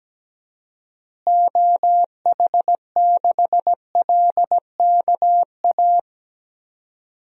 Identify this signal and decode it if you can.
Morse code, heard as a keyed tone, about 17 wpm, 700 Hz. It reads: OH6LKA
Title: DXCC-100 Call Signs - 6 Characters - Repeated 3x - 17wpm